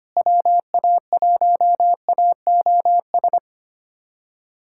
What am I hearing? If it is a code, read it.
Morse code, heard as a keyed tone, about 25 wpm, 700 Hz. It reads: WA1AOS
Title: USA Callsign DE Callsign - 6 Characters - 25wpm